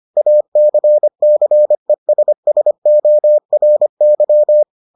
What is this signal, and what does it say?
Morse code, heard as a keyed tone, about 25 wpm, 600 Hz. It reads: ACCESSORY